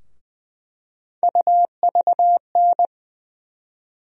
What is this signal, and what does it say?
Morse code, heard as a keyed tone, about 20 wpm, 700 Hz. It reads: UVN
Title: USA Call Sign Postfixes - 3 Letters 20wpm